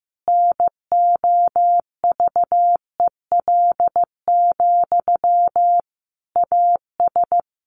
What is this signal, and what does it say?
Morse code, heard as a keyed tone, about 15 wpm, 700 Hz. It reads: NOVEL, AS
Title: Sentences from Top 1300 Words 15wpm